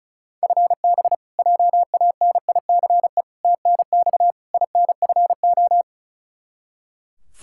Morse code, heard as a keyed tone, about 36 wpm, 700 Hz. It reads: FB JANICE TNX INFO
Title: K1USN SST QSO Practice 35wpm